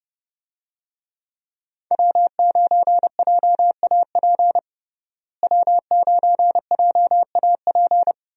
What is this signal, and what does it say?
Morse code, heard as a keyed tone, about 30 wpm, 700 Hz. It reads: W9JAP W9JAP
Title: USA Call Signs - 5 Characters - Repeated 3x - 30wpm